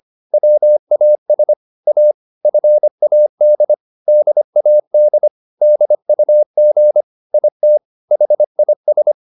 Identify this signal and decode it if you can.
Morse code, heard as a keyed tone, about 25 wpm, 600 Hz. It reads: WAS A FAD DAD DUG IT HIS